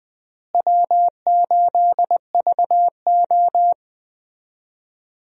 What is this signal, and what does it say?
Morse code, heard as a keyed tone, about 20 wpm, 700 Hz. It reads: W8VO
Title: USA Call Signs - 4 Characters - Repeated 2x - 20wpm